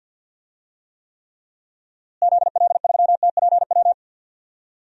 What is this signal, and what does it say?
Morse code, heard as a keyed tone, about 52 wpm, 700 Hz. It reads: ZL3TPW
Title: DXCC-100 Call Signs - 6 Characters - Repeated 3x - 50wpm